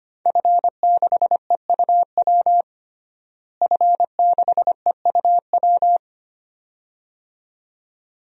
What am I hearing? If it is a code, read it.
Morse code, heard as a keyed tone, about 25 wpm, 700 Hz. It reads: F6EUW F6EUW
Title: DXCC-100 Call Signs - 5 Characters - Repeated 2x - 25wpm